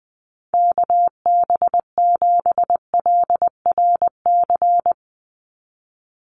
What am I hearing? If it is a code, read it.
Morse code, heard as a keyed tone, about 20 wpm, 700 Hz. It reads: KB7LRC
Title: USA Call Signs 20wpm